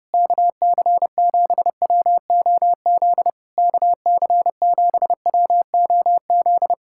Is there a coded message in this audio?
Yes: Morse code from 0.1 to 6.7 s, heard as a keyed tone, about 30 wpm, 700 Hz. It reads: KC7WOZ KC7WOZ